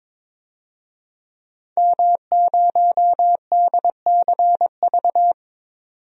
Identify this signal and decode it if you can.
Morse code, heard as a keyed tone, about 22 wpm, 700 Hz. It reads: M0DCV
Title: DXCC-100 Call Signs - 5 Characters - 22wpm